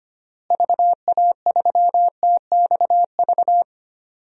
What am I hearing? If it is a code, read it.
Morse code, heard as a keyed tone, about 25 wpm, 700 Hz. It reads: VA3TXV
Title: DXCC-100 Call Signs - 6 Characters - Repeated 2x - 25wpm